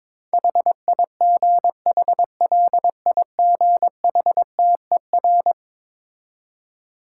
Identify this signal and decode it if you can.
Morse code, heard as a keyed tone, about 22 wpm, 690 Hz. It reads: HIGHLIGHTER
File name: Difficult Words with Hints - 22wpm